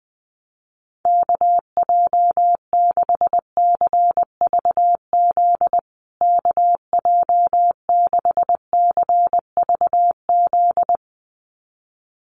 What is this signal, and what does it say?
Morse code, heard as a keyed tone, about 20 wpm, 700 Hz. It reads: KJ6CVZ KJ6CVZ
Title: USA Call Signs Repeated 2x 20wpm